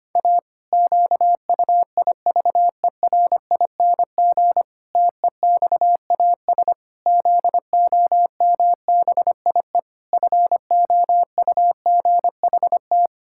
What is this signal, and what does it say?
Morse code, heard as a keyed tone, about 25 wpm, 700 Hz. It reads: A QUIVERING TEXAS ZOMBIE FOUGHT
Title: Pangram Challenge v2 - 25wpm